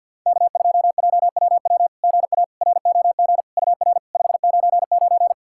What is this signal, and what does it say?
Morse code, heard as a keyed tone, about 50 wpm, 690 Hz. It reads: K2JWW GA ROG UR 599
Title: SOTA QSO Practice 50wpm